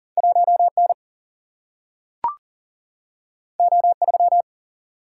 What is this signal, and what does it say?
Morse code, heard as a keyed tone, about 42 wpm, 700 Hz. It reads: 1N O3
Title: Single-Letter Single-Number 40wpm